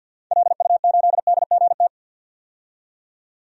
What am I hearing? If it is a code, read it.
Morse code, heard as a keyed tone, about 50 wpm, 700 Hz. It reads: LU8DGT
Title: DXCC-100 Call Signs - 6 Characters - 50wpm